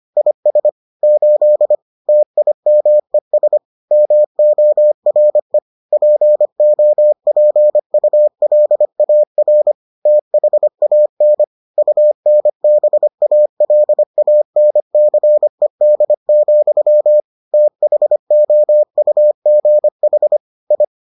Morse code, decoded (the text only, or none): IS 8 TIMES MORE POPULAR THAN UNBALANCED, THOUGH I